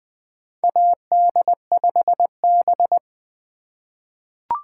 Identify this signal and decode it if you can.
Morse code, heard as a keyed tone, about 20 wpm, 700 Hz. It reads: AD5B
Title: USA Call Signs - 4 Characters - 6x Character Spacing - 20wpm